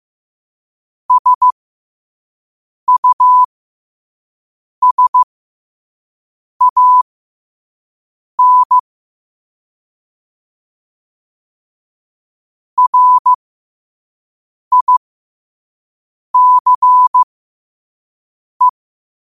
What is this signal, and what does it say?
Morse code, heard as a keyed tone, about 15 wpm, spaced as for 5 wpm, 1000 Hz. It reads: SUSAN RICE